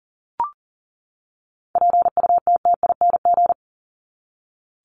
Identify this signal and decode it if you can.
Morse code, heard as a keyed tone, about 40 wpm, 700 Hz. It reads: PUTTING
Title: ING Suffix - Top 100 Words with Base Words 40wpm